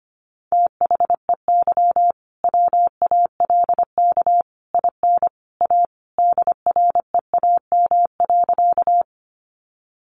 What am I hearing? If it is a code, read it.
Morse code, heard as a keyed tone, about 25 wpm, 700 Hz. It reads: THEY WALK IN A DREAM.
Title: Sentences from Top 1500 Words 25wpm